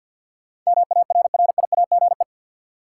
Morse code, heard as a keyed tone, about 53 wpm, 700 Hz. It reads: MARRIAGE
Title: Sentences from Top 800 Words - New Word Drill - 50wpm